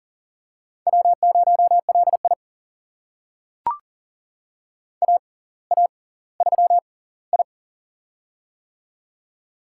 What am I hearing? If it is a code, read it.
Morse code, heard as a keyed tone, about 39 wpm, spaced as for 20 wpm, 690 Hz. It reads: W0LI AA3I
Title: USA Call Signs - 4 Characters - 6x Character Spacing - 40wpm